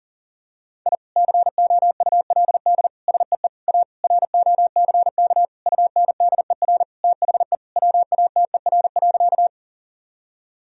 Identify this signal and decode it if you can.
Morse code, heard as a keyed tone, about 37 wpm, 700 Hz. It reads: I COULD SEE A ROCK UNDER THE WATER.
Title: Sentences from Top 1000 Words - Review Part 2 - 40wpm